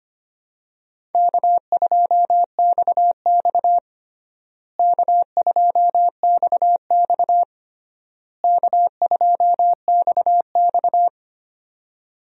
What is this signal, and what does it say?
Morse code, heard as a keyed tone, about 25 wpm, 700 Hz. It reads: K2XX K2XX K2XX
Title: USA Call Signs - 4 Characters - Repeated 3x - 25wpm